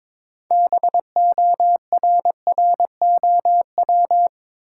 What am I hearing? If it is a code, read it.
Morse code, heard as a keyed tone, about 22 wpm, 700 Hz. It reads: BORROW
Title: Sentences from Top 1500 Words - New Word Drill Alt - 22wpm